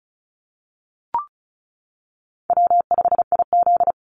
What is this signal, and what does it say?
Morse code, heard as a keyed tone, about 36 wpm, 700 Hz. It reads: W5IZ
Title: Call Signs - 35wpm